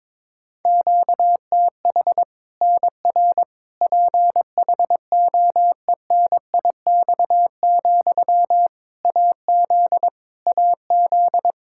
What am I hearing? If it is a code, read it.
Morse code, heard as a keyed tone, about 22 wpm, 700 Hz. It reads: QTH NR PHOENIX, AZ AZ
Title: Practice QSO - Exchange 1 - Beginning - 22wpm